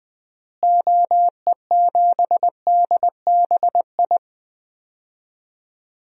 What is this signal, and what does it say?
Morse code, heard as a keyed tone, about 20 wpm, 700 Hz. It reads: OE7DBI
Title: DXCC-100 Call Signs - 6 Characters - Repeated 3x - 20wpm